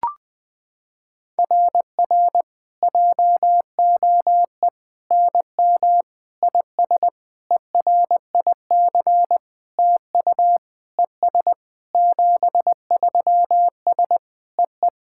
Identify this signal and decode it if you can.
Morse code, heard as a keyed tone, about 20 wpm, 700 Hz. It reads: RR JOE NM IS ERIC TU ES 73S EE
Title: SOTA QSO Practice 20wpm